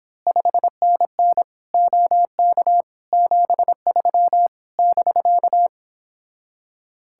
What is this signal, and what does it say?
Morse code, heard as a keyed tone, about 26 wpm, 700 Hz. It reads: 5NN OK 73 <BK>
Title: Speed-Racing Common QSO Phrases - 22wpm